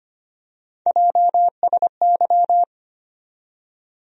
Morse code, heard as a keyed tone, about 25 wpm, 700 Hz. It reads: JSY
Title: USA Call Sign Postfixes - 3 Letters 25wpm